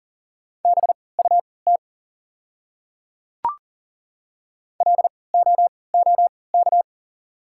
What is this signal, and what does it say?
Morse code, heard as a keyed tone, about 40 wpm, spaced as for 23 wpm, 690 Hz. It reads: BUT LOOK
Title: 3x Character Spacing - Top 200 Words - 40wpm